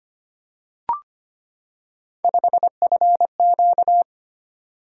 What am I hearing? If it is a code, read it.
Morse code, heard as a keyed tone, about 25 wpm, 700 Hz. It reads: 5FQ
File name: Three Letter-Number Combinations - 25wpm